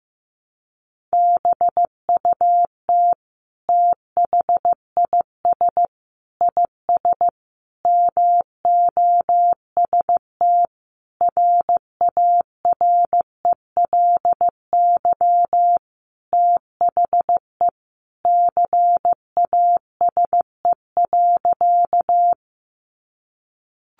Morse code, heard as a keyed tone, about 15 wpm, 700 Hz. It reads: BUT THIS IS MOST RARELY THE CASE.
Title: Sentences from Top 1400 Words 15wpm